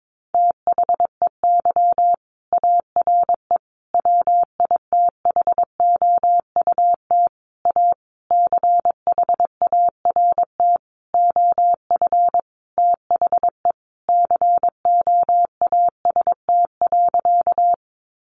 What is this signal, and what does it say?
Morse code, heard as a keyed tone, about 22 wpm, 700 Hz. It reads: THEY ARE WITHOUT A CHART OF THE COAST.
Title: Sentences from Top 1300 Words 22wpm